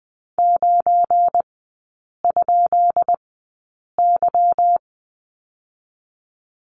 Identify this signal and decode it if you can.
Morse code, heard as a keyed tone, about 20 wpm, spaced as for 8 wpm, 700 Hz. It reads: 9?Y
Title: 172 Lesson 28e - Number 9 - Three Characters 20wpm